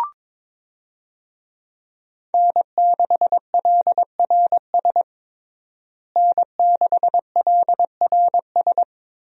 Morse code, heard as a keyed tone, about 22 wpm, 700 Hz. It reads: N6LRS N6LRS
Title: USA Call Signs - 5 Characters - Repeated 3x - 22wpm